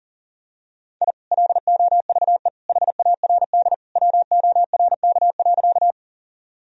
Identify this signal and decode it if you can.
Morse code, heard as a keyed tone, about 37 wpm, 700 Hz. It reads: I LOVE HARD WORK.
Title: Sentences from Top 1000 Words - Review Part 1 - 40wpm